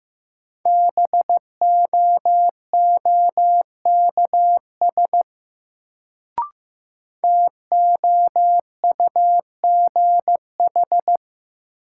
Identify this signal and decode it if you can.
Morse code, heard as a keyed tone, about 15 wpm, 700 Hz. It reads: BOOKS TOUGH
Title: Sentences from Top 1200 Words - New Word Drill Alt - 15wpm